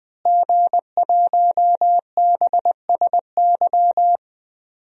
Morse code, heard as a keyed tone, about 20 wpm, 700 Hz. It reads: G1BSY
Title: DXCC-100 Call Signs - 5 Characters - Repeated 3x - 20wpm